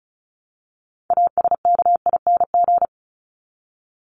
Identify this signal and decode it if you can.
Morse code, heard as a keyed tone, about 34 wpm, 700 Hz. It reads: ASKING
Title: ING Suffix - Top 100 Words with Base Words 35wpm